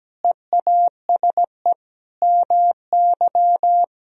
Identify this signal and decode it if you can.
Morse code, heard as a keyed tone, about 17 wpm, 700 Hz. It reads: EASE MY
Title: Sentences from Top 1000 Words - Review Part 1 - 17wpm